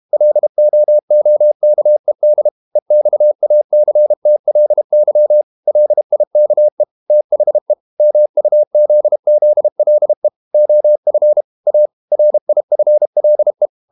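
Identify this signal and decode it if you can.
Morse code, heard as a keyed tone, about 32 wpm, 600 Hz. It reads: LOOKED EXACTLY LIKE THE MUZZLE OF A RIFLE
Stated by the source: Morse code practice at 32 wpm from CW med Gnister, reading Ken Follett's novel Never in English.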